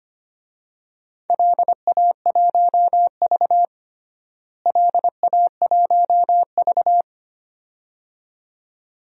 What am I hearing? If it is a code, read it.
Morse code, heard as a keyed tone, about 25 wpm, 700 Hz. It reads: LA1V LA1V
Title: DXCC-100 Call Signs - 4 Characters - Repeated 2x - 25wpm